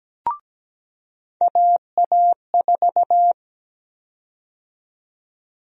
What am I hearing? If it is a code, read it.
Morse code, heard as a keyed tone, about 17 wpm, 700 Hz. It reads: AA4
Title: USA Call Sign Prefixes - 17wpm